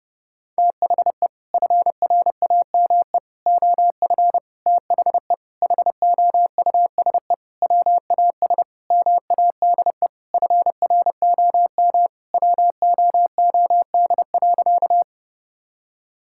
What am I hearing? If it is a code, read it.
Morse code, heard as a keyed tone, about 32 wpm, 700 Hz. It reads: THE FRAME OF THE HOUSE WAS MADE FROM WOOD.
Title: Sentences from Top 1000 Words - Review Part 6 - 30wpm